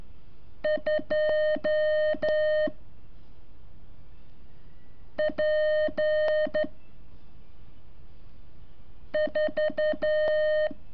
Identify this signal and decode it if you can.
Morse code, heard as a keyed tone, about 12 wpm, 610 Hz. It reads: 2 P 4
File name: curso / ejercicio 24